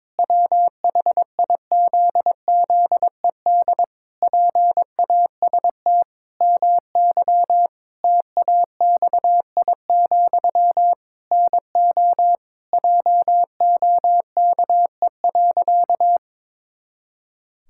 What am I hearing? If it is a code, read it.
Morse code, heard as a keyed tone, about 22 wpm, 700 Hz. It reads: WHIZZED PAST MY TAXI, NO JOKE.
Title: Pangram Challenge v2 - 22wpm